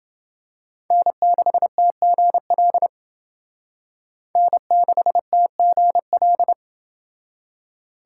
Morse code, heard as a keyed tone, about 28 wpm, 700 Hz. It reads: N6TGL N6TGL